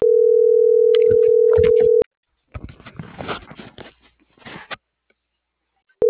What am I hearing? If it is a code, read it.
Ringback tone (North America), 440 + 480 Hz.